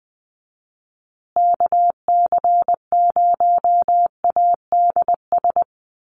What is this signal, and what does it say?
Morse code, heard as a keyed tone, about 20 wpm, 700 Hz. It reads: KC0ADS
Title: USA Call Signs Repeated 2x 20wpm